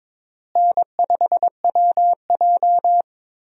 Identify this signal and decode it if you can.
Morse code, heard as a keyed tone, about 21 wpm, 700 Hz. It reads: N5WJ
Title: USA Call Signs - 4 Characters - Repeated 2x - 22wpm